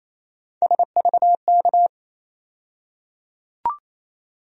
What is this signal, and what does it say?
Morse code, heard as a keyed tone, about 28 wpm, 700 Hz. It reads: SVK
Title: USA Call Sign Postfixes - 3 Letters 28wpm